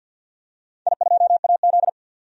Morse code, heard as a keyed tone, about 51 wpm, 700 Hz. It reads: I2AZ
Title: DXCC-100 Call Signs - 4 Characters - 50wpm